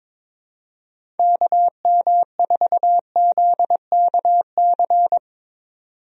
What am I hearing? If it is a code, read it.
Morse code, heard as a keyed tone, about 22 wpm, 700 Hz. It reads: KM4ZKC
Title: USA Call Signs - 6 Characters - 22wpm